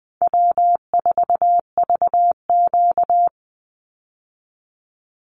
Morse code, heard as a keyed tone, about 20 wpm, 700 Hz. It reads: W4VQ
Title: Easy Call Signs - 20wpm